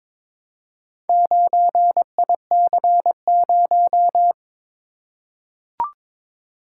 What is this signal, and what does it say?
Morse code, heard as a keyed tone, about 22 wpm, 700 Hz. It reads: 9IC0
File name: practice / 22wpm / Four Letter-Number Combinations - 22wpm